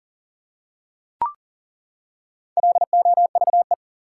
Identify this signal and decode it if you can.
Morse code, heard as a keyed tone, about 40 wpm, 700 Hz. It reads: LOVE
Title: Four Letter Words - 40wpm